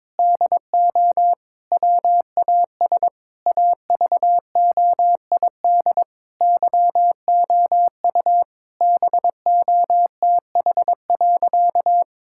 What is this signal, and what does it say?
Morse code, heard as a keyed tone, about 22 wpm, 700 Hz. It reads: DO WAS AVOID YOU BOTH.
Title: Sentences from Top 1000 Words - Review Part 6 - 22wpm